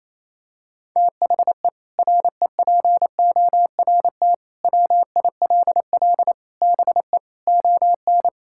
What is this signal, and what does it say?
Morse code, heard as a keyed tone, about 28 wpm, 700 Hz. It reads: THE REPORT WILL BE ON
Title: Sentences from Top 1000 Words - Review Part 6 - 28wpm